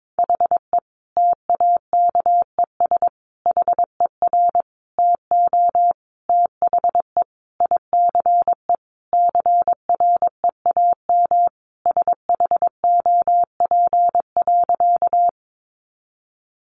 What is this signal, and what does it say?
Morse code, heard as a keyed tone, about 22 wpm, 700 Hz. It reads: HE TAKES HER TO THE ICE CREAM SHOP.
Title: Additional Easy Phrases 22wpm